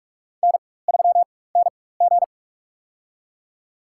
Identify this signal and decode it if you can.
Morse code, heard as a keyed tone, about 46 wpm, spaced as for 21 wpm, 700 Hz. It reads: N3NG
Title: USA Call Signs - 4 Characters - 4x Character Spacing - 45wpm